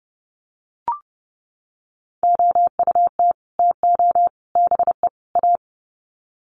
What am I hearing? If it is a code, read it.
Morse code, heard as a keyed tone, about 30 wpm, 700 Hz. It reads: OUT TO BE A
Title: Sets of 4 Words 30wpm